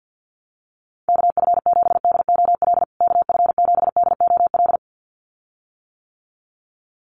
Morse code, heard as a keyed tone, about 50 wpm, 700 Hz. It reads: KF7DOL KF7DOL
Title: USA Call Signs Repeated 2x 50wpm